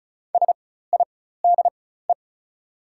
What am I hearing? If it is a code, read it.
Morse code, heard as a keyed tone, about 36 wpm, spaced as for 16 wpm, 690 Hz. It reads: SIDE
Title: 4x Character Spacing - Top 100 Words - 35wpm